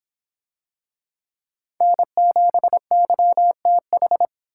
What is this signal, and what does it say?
Morse code, heard as a keyed tone, about 23 wpm, 700 Hz. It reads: N7YTH